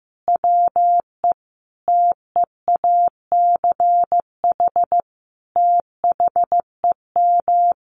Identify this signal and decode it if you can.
Morse code, heard as a keyed tone, about 15 wpm, 700 Hz. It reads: WE TEACH THEM
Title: Sentences from Top 1300 Words 15wpm